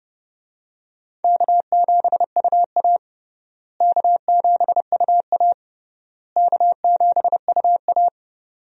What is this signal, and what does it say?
Morse code, heard as a keyed tone, about 30 wpm, 700 Hz. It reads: K7UA K7UA K7UA
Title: USA Call Signs - 4 Characters - Repeated 3x - 30wpm